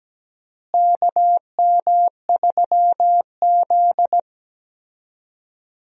Morse code, heard as a keyed tone, about 17 wpm, 700 Hz. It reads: KM3Z
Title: USA Call Signs - 4 Characters - Repeated 2x - 17wpm